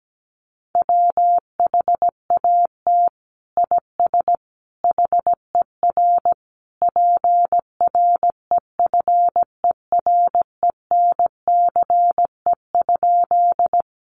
Morse code, heard as a keyed tone, about 17 wpm, 700 Hz. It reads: WHAT IS HER PREFERENCE?
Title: Sentences from Top 1500 Words 17wpm